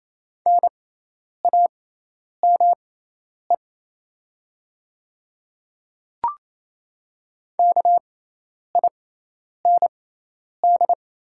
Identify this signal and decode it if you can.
Morse code, heard as a keyed tone, about 28 wpm, spaced as for 10 wpm, 700 Hz. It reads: NAME KIND
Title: 6x Character Spacing - Top 300 Words - 28wpm